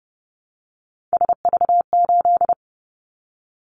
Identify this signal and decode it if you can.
Morse code, heard as a keyed tone, about 30 wpm, 700 Hz. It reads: SV8
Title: Callsign Prefixes for Top 50 DXCC - 30wpm